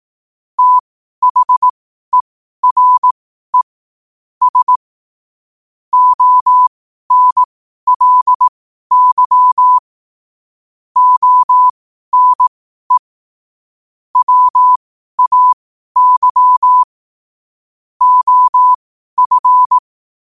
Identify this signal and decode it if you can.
Morse code, heard as a keyed tone, about 18 wpm, spaced as for 12 wpm, 1000 Hz. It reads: THERE S ONLY ONE WAY OF
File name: qotdfiles / 12wpm